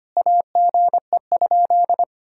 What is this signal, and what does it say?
Morse code, heard as a keyed tone, about 25 wpm, 700 Hz. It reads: AGE?
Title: Sentences from Top 1600 Words 25wpm